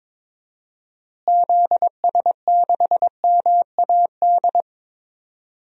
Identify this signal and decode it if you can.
Morse code, heard as a keyed tone, about 22 wpm, 700 Hz. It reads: ZS6MAD
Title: DXCC-100 Call Signs - 6 Characters - Repeated 2x - 22wpm